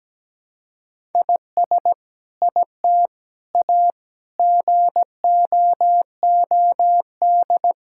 Morse code, heard as a keyed tone, about 17 wpm, 700 Hz. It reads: IS IT A GOOD
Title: Sentences from Top 1000 Words - Review Part 5 - 17wpm